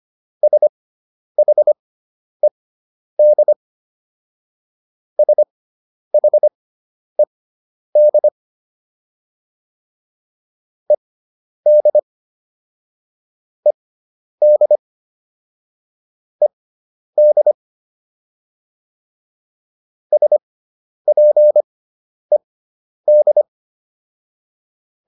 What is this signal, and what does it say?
Morse code, heard as a keyed tone, about 25 wpm, spaced as for 10 wpm, 600 Hz. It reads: SHED SHED ED ED ED SPED